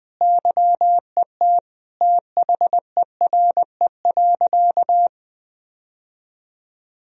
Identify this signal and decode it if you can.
Morse code, heard as a keyed tone, about 20 wpm, 700 Hz. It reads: YET THERE.
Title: Sentences from Top 1000 Words - Review Part 1 - 20wpm